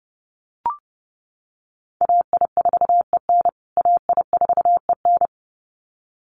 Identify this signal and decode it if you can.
Morse code, heard as a keyed tone, about 30 wpm, 700 Hz. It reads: AI4EN AI4EN
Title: USA Call Signs Repeated 2x 30wpm